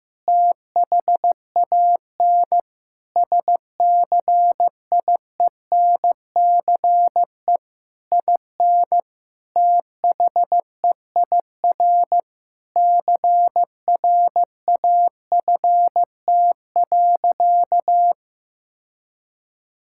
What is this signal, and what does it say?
Morse code, heard as a keyed tone, about 15 wpm, 700 Hz. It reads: THAN SCIENCE IN THEIR CRAFT.
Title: Sentences from Top 1000 Words - Review Part 6 - 15wpm